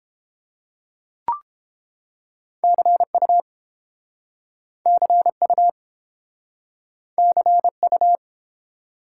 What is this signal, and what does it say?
Morse code, heard as a keyed tone, about 30 wpm, 700 Hz. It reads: CU CU CU